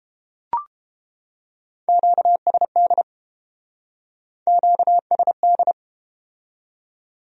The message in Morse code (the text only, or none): QSD QSD